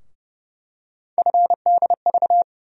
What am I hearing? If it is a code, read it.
Morse code, heard as a keyed tone, about 30 wpm, 700 Hz. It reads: FDV
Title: USA Call Sign Postfixes - 3 Letters 30wpm